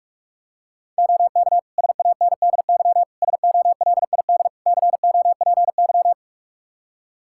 Morse code, heard as a keyed tone, about 46 wpm, 700 Hz. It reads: OK SANDY SOLID COPY
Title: K1USN SST QSO Practice 45wpm